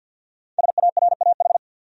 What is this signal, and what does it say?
Morse code, heard as a keyed tone, about 51 wpm, 700 Hz. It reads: SARAH
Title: USA Common Names - 50wpm